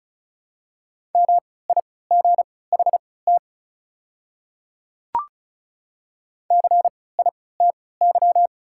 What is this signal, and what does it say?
Morse code, heard as a keyed tone, about 36 wpm, spaced as for 20 wpm, 700 Hz. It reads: MIGHT CITY
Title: 3x Character Spacing - Top 300 Words - 35wpm